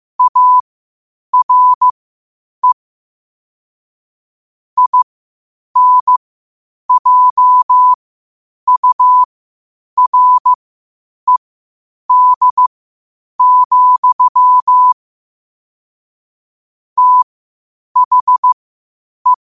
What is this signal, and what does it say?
Morse code, heard as a keyed tone, about 15 wpm, spaced as for 8 wpm, 1000 Hz. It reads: ARE INJURED, THE